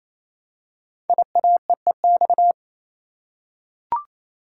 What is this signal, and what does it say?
Morse code, heard as a keyed tone, about 27 wpm, 700 Hz. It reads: IAEEX
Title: Code Groups - 5 Letter:Numbers - 28wpm